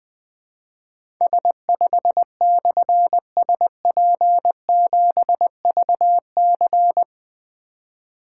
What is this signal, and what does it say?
Morse code, heard as a keyed tone, about 20 wpm, 700 Hz. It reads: S5/SP7VC
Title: Difficult Call Signs 20wpm